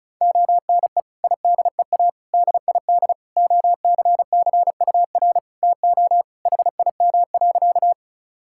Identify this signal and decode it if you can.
Morse code, heard as a keyed tone, about 36 wpm, 700 Hz. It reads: ONE IDEA DID OCCUR TO HIM.
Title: Sentences from Top 1000 Words - Review Part 6 - 35wpm